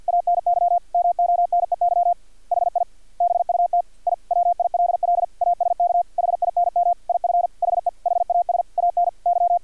Morse code, heard as a keyed tone, about 50 wpm, 700 Hz. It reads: ANY MONEY HI BUT I WILL ASK HINM IF HE HAS ANY
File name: archived / CRSnet